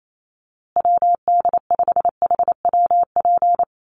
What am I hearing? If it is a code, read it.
Morse code, heard as a keyed tone, about 28 wpm, 700 Hz. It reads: WD5HWP
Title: USA Call Signs Repeated 2x 28wpm